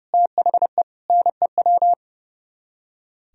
Morse code, heard as a keyed tone, about 36 wpm, 700 Hz. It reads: THE NEW
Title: Speed-Racing - Sets of 3 Words 30wpm